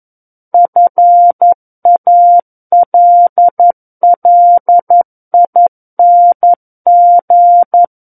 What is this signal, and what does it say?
Morse code, heard as a keyed tone, about 11 wpm, 700 Hz. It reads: FALLING
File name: 11 WPM